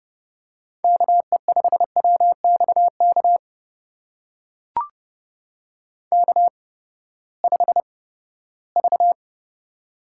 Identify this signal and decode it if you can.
Morse code, heard as a keyed tone, about 30 wpm, spaced as for 17 wpm, 700 Hz. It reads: KE5WXK K5V
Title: USA Call Signs - 8x Character Spacing - 30wpm